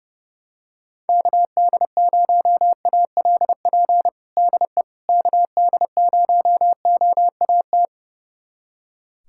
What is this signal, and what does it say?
Morse code, heard as a keyed tone, about 30 wpm, 700 Hz. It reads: KD0ALP DE KD0OAT
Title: USA Callsign DE Callsign 30wpm